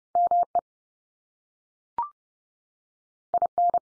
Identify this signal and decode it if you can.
Morse code, heard as a keyed tone, about 30 wpm, 700 Hz. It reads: ME IN
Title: US States 30WPM